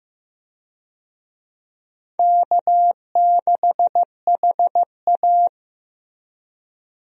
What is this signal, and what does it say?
Morse code, heard as a keyed tone, about 15 wpm, 700 Hz. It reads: K6HA
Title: USA Call Signs - 4 Characters - Repeated 3x - 15wpm